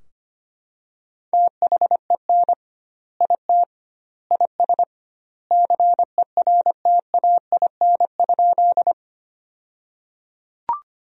Morse code, heard as a keyed tone, about 25 wpm, 700 Hz. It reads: THEN IT IS CERTAIN?
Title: 2x Word-Spacing - Sentences from Top 500 Words - 25wpm